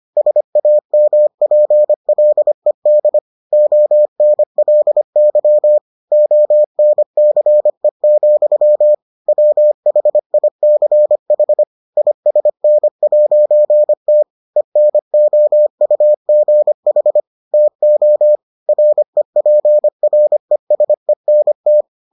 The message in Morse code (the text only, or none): SAMPLED ONLY ONCE, WHICH ISN'T ENOUGH TO REPRESENT